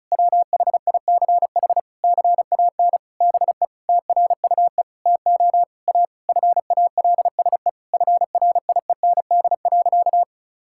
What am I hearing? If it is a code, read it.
Morse code, heard as a keyed tone, about 34 wpm, 700 Hz. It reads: WHICH CAN BE TRUE TO A FALSE FRIEND.
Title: Sentences from Top 1600 Words 35wpm